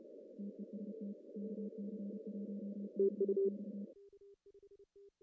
Morse code, heard as a keyed tone, about 33 wpm, 200 Hz. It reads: TEST KC1XX
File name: CW / cqww-cw-k5tr